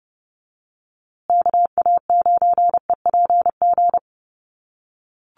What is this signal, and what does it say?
Morse code, heard as a keyed tone, about 30 wpm, 700 Hz. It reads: KA9EPG
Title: USA Call Signs Repeated 3x 30wpm